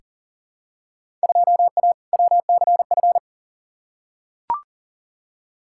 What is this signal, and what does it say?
Morse code, heard as a keyed tone, about 40 wpm, 700 Hz. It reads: 2A WCF
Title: Field Day Exchanges - 40wpm